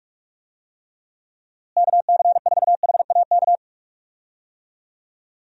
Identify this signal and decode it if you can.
Morse code, heard as a keyed tone, about 44 wpm, 700 Hz. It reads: KC4HAK
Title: USA Call Signs - 6 Characters - Repeated 2x - 45wpm